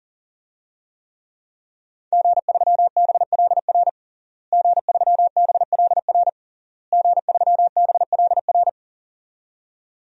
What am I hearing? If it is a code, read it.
Morse code, heard as a keyed tone, about 40 wpm, 700 Hz. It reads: G3BLR G3BLR G3BLR
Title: DXCC-100 Call Signs - 5 Characters - Repeated 3x - 40wpm